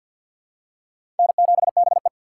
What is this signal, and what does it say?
Morse code, heard as a keyed tone, about 47 wpm, 700 Hz. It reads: N7BE
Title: USA Call Signs - 6x Character Spacing - 50wpm